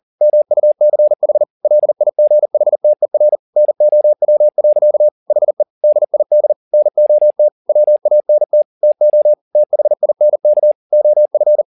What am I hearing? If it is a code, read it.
Morse code, heard as a keyed tone, about 40 wpm, 600 Hz. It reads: MUCH LIGHTER NOW. HE DID NOT WANT TO THINK OF
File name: the old man and the sea 2 40wpm